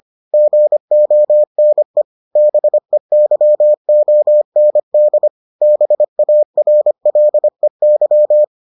Morse code, heard as a keyed tone, about 25 wpm, 600 Hz. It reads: GONE BEYOND BARLEY